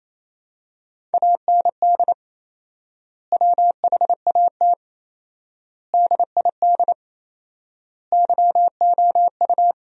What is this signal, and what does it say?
Morse code, heard as a keyed tone, about 28 wpm, 700 Hz. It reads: AND WHAT DID YOU
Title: 4x Word-Spacing - Sentences from Top 100 Words - 28wpm